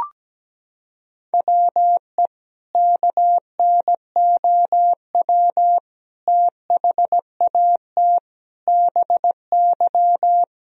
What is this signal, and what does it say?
Morse code, heard as a keyed tone, about 17 wpm, 700 Hz. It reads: WE KNOW THAT BY
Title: Sentences from Top 1000 Words - Review Part 3 - 17wpm